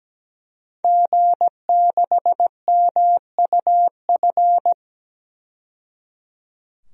Morse code, heard as a keyed tone, about 17 wpm, 700 Hz. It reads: G6MUF
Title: DXCC-100 Call Signs - 5 Characters - 17wpm